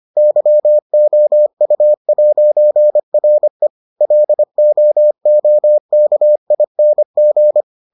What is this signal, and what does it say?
Morse code, heard as a keyed tone, about 25 wpm, 600 Hz. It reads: YOU'RE LOOKING